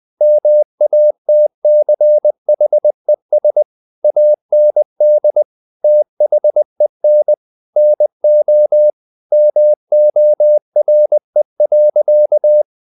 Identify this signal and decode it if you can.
Morse code, heard as a keyed tone, about 20 wpm, 600 Hz. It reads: MATCHES AND THEN NO MORE.
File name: the old man and the sea part 2 20wpm